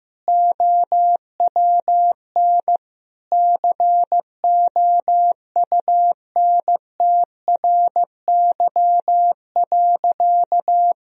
Morse code, heard as a keyed tone, about 15 wpm, 700 Hz. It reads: OWN COUNTRY.
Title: Sentences from Top 1000 Words - Review Part 1 - 15wpm